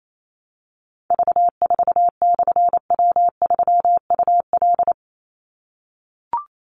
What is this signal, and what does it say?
Morse code, heard as a keyed tone, about 28 wpm, 700 Hz. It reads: V4/W3UL
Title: Call Signs - 28wpm